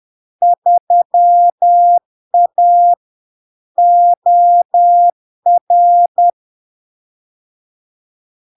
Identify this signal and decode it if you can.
Morse code, heard as a keyed tone, about 10 wpm, 700 Hz. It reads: 3A OR
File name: LARK-FD-2022-Log-in-CW-10wpm-700-Hz